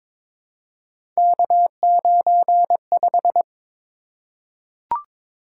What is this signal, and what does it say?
Morse code, heard as a keyed tone, about 22 wpm, 700 Hz. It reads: K95
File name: practice / 22wpm / Three Letter-Number Combinations - 22wpm